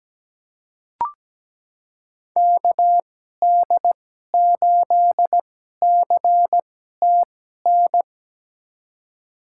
Morse code, heard as a keyed tone, about 17 wpm, spaced as for 12 wpm, 700 Hz. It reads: KD8CTN
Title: USA Call Signs - 2x Character Spacing - 17wpm